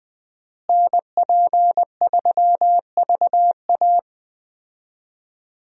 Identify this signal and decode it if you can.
Morse code, heard as a keyed tone, about 21 wpm, 700 Hz. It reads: NP3VA
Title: USA Call Signs - 5 Characters - Repeated 2x - 20wpm